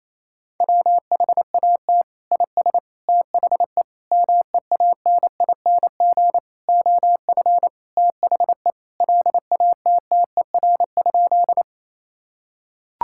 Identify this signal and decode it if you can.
Morse code, heard as a keyed tone, about 28 wpm, 700 Hz. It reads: WHAT IS THE MEANING OF THE LATTER?
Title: Sentences from Top 1600 Words 28wpm